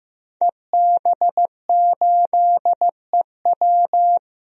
Morse code, heard as a keyed tone, about 15 wpm, 700 Hz. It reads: EB8EW
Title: DXCC-100 Call Signs - 5 Characters - 15wpm